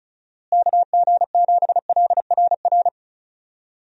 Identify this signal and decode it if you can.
Morse code, heard as a keyed tone, about 34 wpm, 700 Hz. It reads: KG7LRR
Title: USA Call Signs - 6 Characters - Repeated 3x - 35wpm